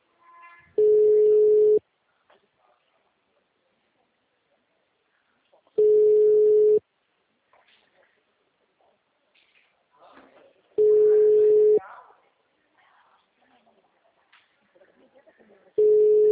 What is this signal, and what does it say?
Ringback tone (Europe, 425 Hz), with speech.
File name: out- -2003- - - .24328